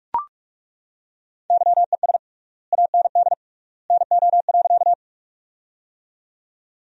Morse code, heard as a keyed tone, about 41 wpm, 700 Hz. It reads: YES AND NO.
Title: 3x Word-Spacing - Sentences from Top 500 Words - 45wpm